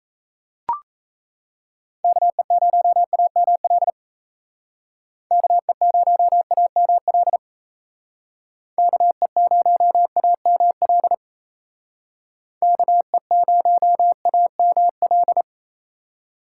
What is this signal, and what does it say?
Morse code, heard as a keyed tone, about 37 wpm, 700 Hz. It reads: KE0AML KE0AML KE0AML KE0AML